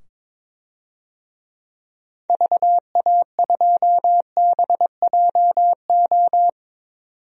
Morse code, heard as a keyed tone, about 22 wpm, 700 Hz. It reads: VA2BJO
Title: DXCC-100 Call Signs - 6 Characters - 22wpm